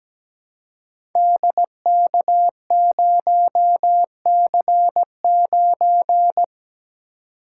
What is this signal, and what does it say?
Morse code, heard as a keyed tone, about 17 wpm, 700 Hz. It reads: DK0C9
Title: Code Groups - 5 Letter:Numbers - 17wpm